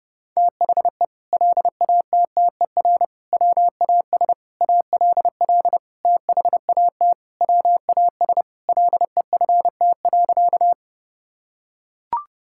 Morse code, heard as a keyed tone, about 28 wpm, 700 Hz. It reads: THE LATTER WAS ALL THAT WAS LEFT.
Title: Sentences from Top 1600 Words 30wpm